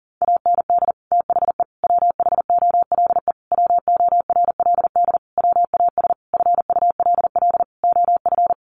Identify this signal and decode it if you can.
Morse code, heard as a keyed tone, about 40 wpm, 700 Hz. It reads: AND THE WHOLE WORLD WAS FULL OF
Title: Sentences from Top 1300 Words 40wpm